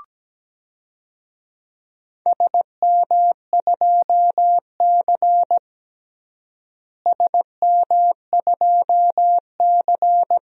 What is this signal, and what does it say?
Morse code, heard as a keyed tone, about 17 wpm, 700 Hz. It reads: SM2C SM2C
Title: DXCC-100 Call Signs - 4 Characters - Repeated 2x - 17wpm